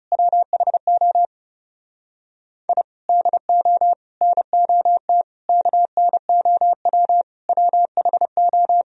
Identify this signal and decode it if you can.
Morse code, heard as a keyed tone, about 36 wpm, 700 Hz. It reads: WHO I DO NOT KNOW WHO
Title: Speed-Racing - Sets of 5 Words 30wpm